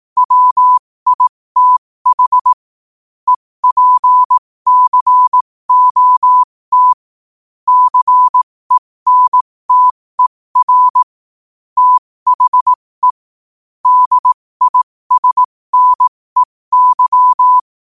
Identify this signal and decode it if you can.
Morse code, heard as a keyed tone, about 18 wpm, spaced as for 15 wpm, 1000 Hz. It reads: WITH EPCOT CENTER THE DISNEY
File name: qotdfiles / 15wpm